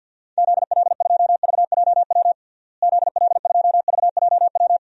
Morse code, heard as a keyed tone, about 49 wpm, 700 Hz. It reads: ZL2VJW ZL2VJW
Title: DXCC-100 Call Signs - 6 Characters - Repeated 3x - 50wpm